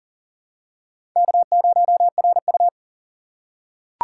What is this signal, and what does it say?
Morse code, heard as a keyed tone, about 40 wpm, 700 Hz. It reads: K0RU
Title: USA Call Signs - 4 Characters - 4x Character Spacing - 40wpm